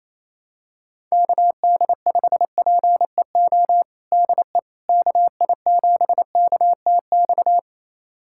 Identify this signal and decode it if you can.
Morse code, heard as a keyed tone, about 28 wpm, 700 Hz. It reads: KD5PEO DE KI7KTX
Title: USA Callsign DE Callsign - 6 Characters - 28wpm